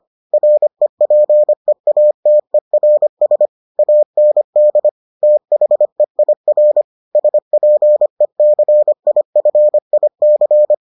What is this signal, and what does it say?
Morse code, heard as a keyed tone, about 25 wpm, 600 Hz. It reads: REPEATERS AND THEIR SPECIFIC